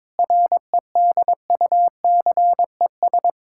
Morse code, heard as a keyed tone, about 22 wpm, 700 Hz. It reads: REDUCES
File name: Difficult Sentences with Hints - 22wpm